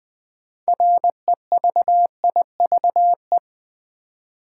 Morse code, heard as a keyed tone, about 20 wpm, 700 Hz. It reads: REVIVE
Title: Build A Word - 3 Letters At A Time - 6-Letter Words - 20wpm